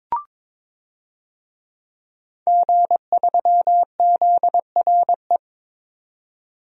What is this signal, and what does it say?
Morse code, heard as a keyed tone, about 21 wpm, 700 Hz. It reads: G3ZRE
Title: DXCC-100 Call Signs - 5 Characters - 22wpm